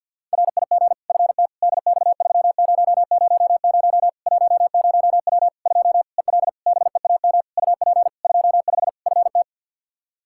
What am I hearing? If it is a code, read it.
Morse code, heard as a keyed tone, about 52 wpm, 700 Hz. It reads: RIG FT DX3000 10W 2 EL BEAM UP 25 FT
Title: Practice QSO - Exchange 2 - Info - 50wpm